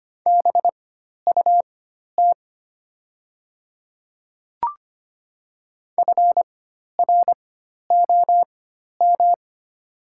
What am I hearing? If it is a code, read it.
Morse code, heard as a keyed tone, about 25 wpm, spaced as for 12 wpm, 700 Hz. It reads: BUT FROM
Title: 4x Character Spacing - Top 100 Words - 25wpm